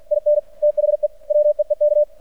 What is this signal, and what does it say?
Morse code, heard as a keyed tone, about 50 wpm, 600 Hz. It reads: IN THE WEEK
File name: archived / CRSnet